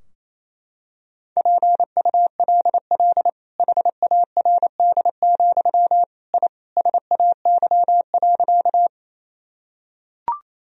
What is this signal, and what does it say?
Morse code, heard as a keyed tone, about 28 wpm, 700 Hz. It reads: PULL HARD, I SAY.
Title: Sentences from Top 1000 Words - Review Part 2 - 28wpm